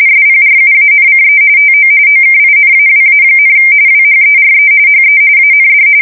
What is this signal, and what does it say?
RTTY, about 45.45 baud. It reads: SPRUANCE TO OKINAWA, WHERE HE AND
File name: fsk170 / USS-New-Jersey